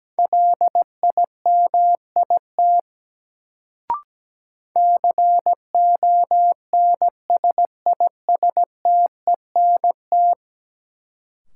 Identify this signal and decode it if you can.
Morse code, heard as a keyed tone, about 17 wpm, 700 Hz. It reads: LIMIT CONSISTENT
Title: Sentences from Top 1500 Words - New Word Drill Alt - 17wpm